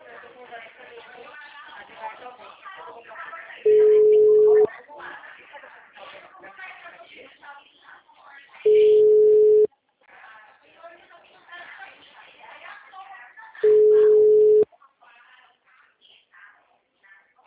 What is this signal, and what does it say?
Ringback tone (Europe, 425 Hz), with speech.